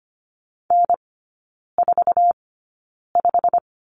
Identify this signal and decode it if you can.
Morse code, heard as a keyed tone, about 25 wpm, spaced as for 9 wpm, 700 Hz. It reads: N45
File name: Lesson 15e - Number 5 - Three Characters 25wpm